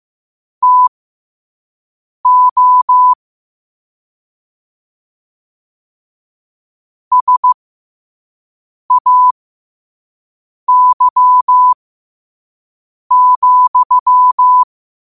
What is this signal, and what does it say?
Morse code, heard as a keyed tone, about 15 wpm, spaced as for 5 wpm, 1000 Hz. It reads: TO SAY,
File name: qotdfiles / 5wpm